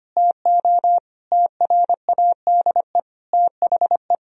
Morse code, heard as a keyed tone, about 25 wpm, 700 Hz. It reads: TO TRADE THE
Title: Sentences from Top 1000 Words - Review Part 4 - 25wpm